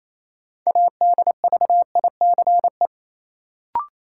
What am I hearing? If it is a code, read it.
Morse code, heard as a keyed tone, about 28 wpm, 700 Hz. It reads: ADVICE
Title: Sentences from Top 1100 Words - New Word Drill - 28wpm